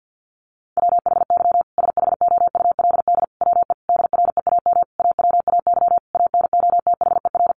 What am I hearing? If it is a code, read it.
Morse code, heard as a keyed tone, about 50 wpm, 700 Hz. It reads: WHY SHOULD WE DREAM AWAY ANOTHER
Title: Sentences from Top 1500 Words 50wpm